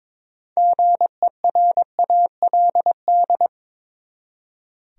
Morse code, heard as a keyed tone, about 22 wpm, 700 Hz. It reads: GERALD
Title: USA Common Names - 22wpm